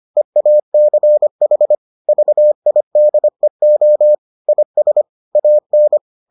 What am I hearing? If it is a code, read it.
Morse code, heard as a keyed tone, about 25 wpm, 600 Hz. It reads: EACH VIDEO IS AN